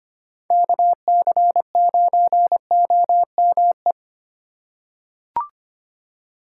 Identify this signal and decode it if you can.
Morse code, heard as a keyed tone, about 25 wpm, 700 Hz. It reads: KC9OME
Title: USA Call Signs - 2x Character Spacing - 25wpm